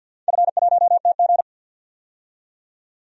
Morse code, heard as a keyed tone, about 51 wpm, 700 Hz. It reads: F1TG
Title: DXCC-100 Call Signs - 4 Characters - 50wpm